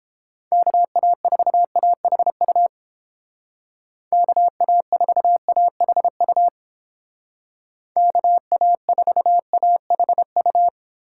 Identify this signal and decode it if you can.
Morse code, heard as a keyed tone, about 30 wpm, 700 Hz. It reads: KA4AHU KA4AHU KA4AHU